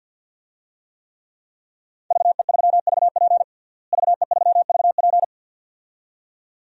Morse code, heard as a keyed tone, about 52 wpm, 700 Hz. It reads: VE3VP VE3VP
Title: DXCC-100 Call Signs - 5 Characters - Repeated 2x - 50wpm